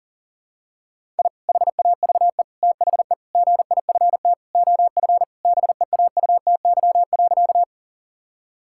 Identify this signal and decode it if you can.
Morse code, heard as a keyed tone, about 40 wpm, 700 Hz. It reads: I HAVE THE GIFT OF BEAUTY.
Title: Sentences from Top 1000 Words - Review Part 6 - 40wpm